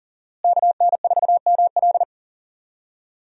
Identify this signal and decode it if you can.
Morse code, heard as a keyed tone, about 40 wpm, 700 Hz. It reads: KN4ML
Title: USA Call Signs - 5 Characters - Repeated 2x - 40wpm